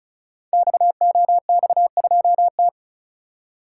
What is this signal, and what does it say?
Morse code, heard as a keyed tone, about 36 wpm, 690 Hz. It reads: XOX2T
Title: Code Groups - 5 Letter:Numbers - 35wpm